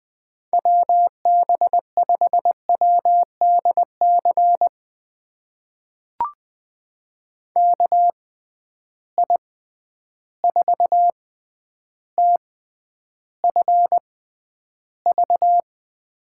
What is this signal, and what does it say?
Morse code, heard as a keyed tone, about 20 wpm, spaced as for 10 wpm, 700 Hz. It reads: WB5WDC KI4TFV
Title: USA Call Signs - 6 Characters - 6x Character Spacing - 20wpm